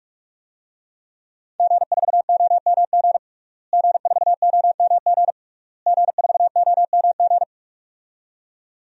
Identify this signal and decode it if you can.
Morse code, heard as a keyed tone, about 46 wpm, 700 Hz. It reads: G4OMG G4OMG G4OMG
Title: DXCC-100 Call Signs - 5 Characters - Repeated 3x - 45wpm